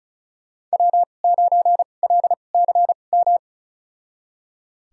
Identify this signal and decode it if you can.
Morse code, heard as a keyed tone, about 36 wpm, spaced as for 25 wpm, 700 Hz. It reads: W9LCM
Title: USA Call Signs - 2x Character Spacing - 35wpm